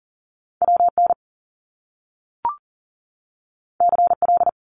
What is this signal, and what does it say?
Morse code, heard as a keyed tone, about 40 wpm, 700 Hz. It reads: WN CL
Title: Top Two Letter Combinations in Words 40wpm